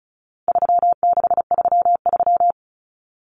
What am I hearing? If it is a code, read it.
Morse code, heard as a keyed tone, about 34 wpm, 700 Hz. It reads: 3633
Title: Numbers - 4-Digit 35wpm